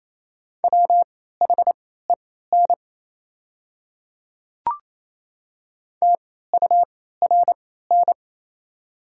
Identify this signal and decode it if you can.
Morse code, heard as a keyed tone, about 27 wpm, spaced as for 16 wpm, 690 Hz. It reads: WHEN TURN
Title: 3x Character Spacing - Top 200 Words - 28wpm